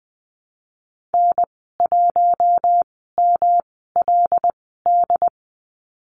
Morse code, heard as a keyed tone, about 20 wpm, spaced as for 15 wpm, 700 Hz. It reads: N1MLD
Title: 122 Lesson 20g - Number 3 - Call Signs 20wpm